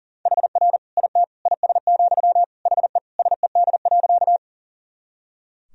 Morse code, heard as a keyed tone, about 40 wpm, 700 Hz. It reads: HR IT IS, HE SED.
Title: Sentences from Top 200 Words and Abbreviations 40wpm